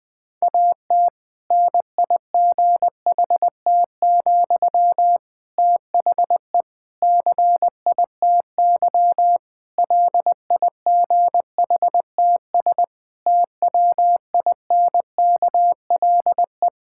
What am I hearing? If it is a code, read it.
Morse code, heard as a keyed tone, about 20 wpm, 700 Hz. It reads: AT NIGHT, THE CITY LIGHTS TWINKLE
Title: Difficult Sentences with Hints - 20wpm